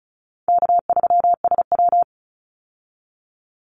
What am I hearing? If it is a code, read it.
Morse code, heard as a keyed tone, about 36 wpm, 700 Hz. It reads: K3SW
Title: Call Signs - 35wpm